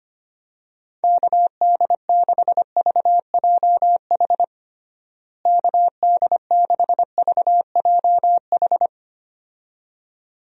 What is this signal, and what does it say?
Morse code, heard as a keyed tone, about 25 wpm, 700 Hz. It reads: KD6VJH KD6VJH
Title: USA Call Signs - 6 Characters - Repeated 2x - 25wpm